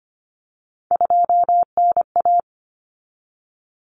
Morse code, heard as a keyed tone, about 25 wpm, 700 Hz. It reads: 2NA
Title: Common Cut Numbers - 3-Digit 25wpm